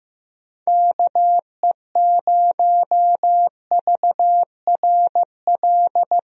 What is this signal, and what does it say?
Morse code, heard as a keyed tone, about 15 wpm, 700 Hz. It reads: KE0VRL
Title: USA Call Signs - 6 Characters - Repeated 3x - 15wpm